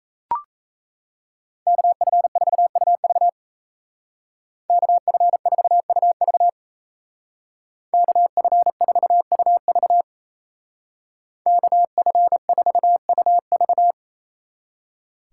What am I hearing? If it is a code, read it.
Morse code, heard as a keyed tone, about 33 wpm, 700 Hz. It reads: KF4UV KF4UV KF4UV KF4UV
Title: Speed Racing - USA 2x2 Call Signs 28wpm